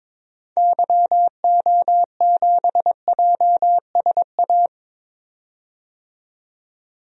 Morse code, heard as a keyed tone, about 22 wpm, 700 Hz. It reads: YO7JSA
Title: DXCC-100 Call Signs - 6 Characters - 22wpm